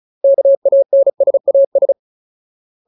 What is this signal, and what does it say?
Morse code, heard as a keyed tone, about 35 wpm, 550 Hz. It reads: KANSAS
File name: Ultimate Word List / 35WPM Ultimate